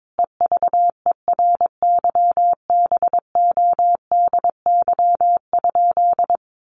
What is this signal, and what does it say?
Morse code, heard as a keyed tone, about 22 wpm, 700 Hz. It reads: EVERYBODY?
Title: Sentences from Top 1400 Words 22wpm